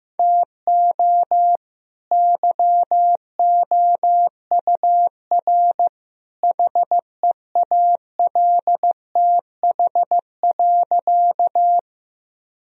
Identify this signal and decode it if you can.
Morse code, heard as a keyed tone, about 15 wpm, 700 Hz. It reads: TO YOUR HEALTH.
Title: Sentences from Top 1000 Words - Review Part 3 - 15wpm